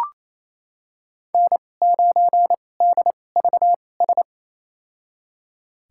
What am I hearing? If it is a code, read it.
Morse code, heard as a keyed tone, about 28 wpm, spaced as for 20 wpm, 700 Hz. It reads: N9DVS
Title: USA Call Signs - 5 Characters - 2x Character Spacing - 28wpm